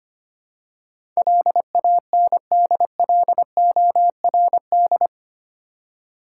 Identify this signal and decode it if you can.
Morse code, heard as a keyed tone, about 25 wpm, 700 Hz. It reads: LANDLORD
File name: Eight Letter Words - 25wpm